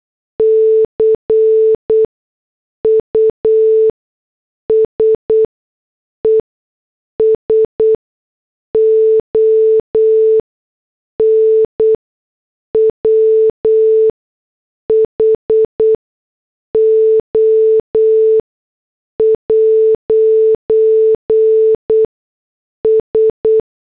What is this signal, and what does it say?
Morse code, heard as a keyed tone, about 8 wpm, spaced as for 6 wpm, 440 Hz. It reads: CUSES ON WHO'S